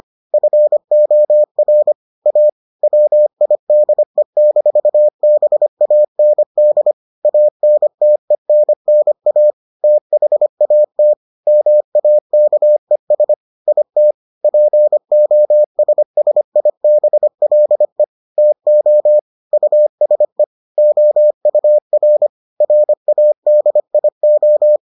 Morse code, heard as a keyed tone, about 25 wpm, 600 Hz. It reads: FOR A WIDE-BAND ANTENNA THAT MAKES IT POSSIBLE TO USE OUR RADIO